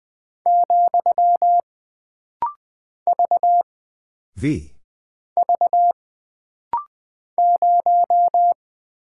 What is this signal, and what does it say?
Morse code, heard as a keyed tone, about 20 wpm, 700 Hz. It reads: , V V 0
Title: Single Character (Letter:Number:Punctuation:Prosign) - Rapid-Fire - 20wpm